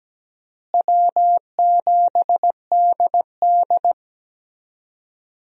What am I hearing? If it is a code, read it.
Morse code, heard as a keyed tone, about 17 wpm, 700 Hz. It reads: W7DD
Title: USA Call Signs - 4 Characters - Repeated 3x - 17wpm